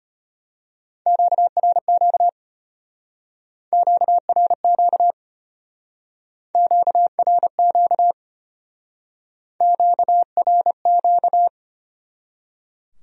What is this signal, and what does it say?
Morse code, heard as a keyed tone, about 31 wpm, 700 Hz. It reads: QRQ QRQ QRQ QRQ
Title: Speed-Racing Q-Codes - 25wpm